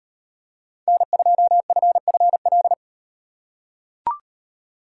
Morse code, heard as a keyed tone, about 38 wpm, 700 Hz. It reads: N2FFL